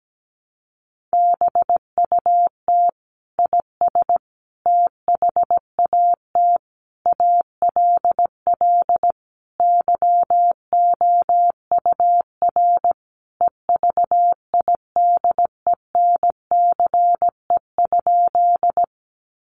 Morse code, heard as a keyed tone, about 17 wpm, 700 Hz. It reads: BUT IS THAT ALL YOUR EVIDENCE?
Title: Sentences from Top 1300 Words 17wpm